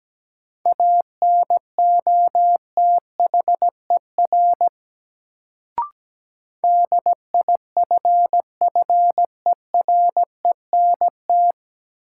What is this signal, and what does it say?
Morse code, heard as a keyed tone, about 17 wpm, 700 Hz. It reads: ANOTHER DIFFERENT
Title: Sentences from Top 600 Words - New Word Drill - 17wpm